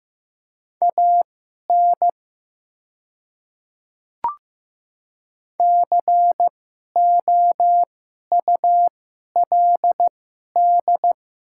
Morse code, heard as a keyed tone, about 15 wpm, spaced as for 11 wpm, 700 Hz. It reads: AN COULD
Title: 2x Character Spacing - Top 300 Words - 15wpm